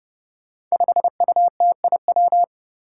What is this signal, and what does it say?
Morse code, heard as a keyed tone, about 30 wpm, 700 Hz. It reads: 5UTIW
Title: Five Challenging Letter-Number Combinations 30wpm